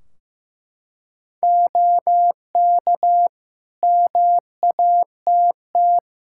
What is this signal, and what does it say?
Morse code, heard as a keyed tone, about 15 wpm, 700 Hz. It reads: OK MATT
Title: Practice QSO - Exchange 2 - Info - 15wpm